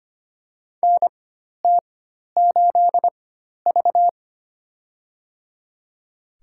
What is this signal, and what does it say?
Morse code, heard as a keyed tone, about 25 wpm, spaced as for 12 wpm, 700 Hz. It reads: NT8V
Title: USA Call Signs - 4 Characters - 4x Character Spacing - 25wpm